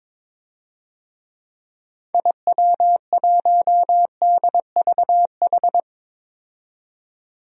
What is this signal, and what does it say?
Morse code, heard as a keyed tone, about 22 wpm, 700 Hz. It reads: IW1DVH
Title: DXCC-100 Call Signs - 6 Characters - 22wpm